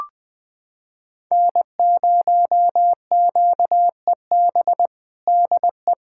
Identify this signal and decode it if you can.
Morse code, heard as a keyed tone, about 20 wpm, 700 Hz. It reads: N0QEB DE
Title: USA Callsign DE Callsign 20wpm